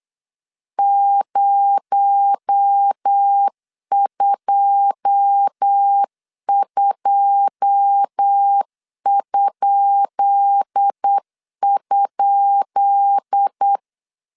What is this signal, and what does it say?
Morse code, heard as a keyed tone, about 8 wpm, 800 Hz. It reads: 022??
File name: LEZIONI / 10 SIGNS / 10B INT 8wpm 3534